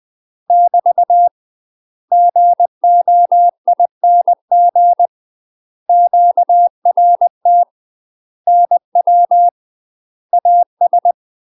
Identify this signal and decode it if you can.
Morse code, heard as a keyed tone, about 20 wpm, 700 Hz. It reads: <BT> GOING QRT NW AS